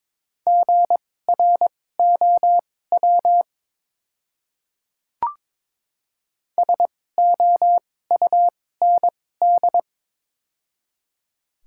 Morse code, heard as a keyed tone, about 22 wpm, spaced as for 16 wpm, 700 Hz. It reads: GROW SOUND
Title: 2x Character Spacing - Top 300 Words - 22wpm